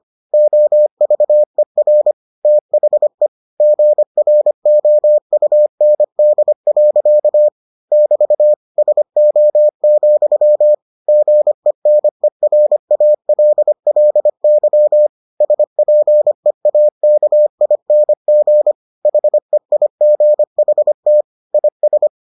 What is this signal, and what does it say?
Morse code, heard as a keyed tone, about 25 wpm, 600 Hz. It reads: OVER THE GROUND. <BT> SO, GENERALLY SPEAKING HEIGHT IS